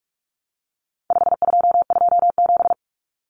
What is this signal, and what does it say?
Morse code, heard as a keyed tone, about 44 wpm, 700 Hz. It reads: 5227
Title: Common Cut Numbers - 4-Digit 45wpm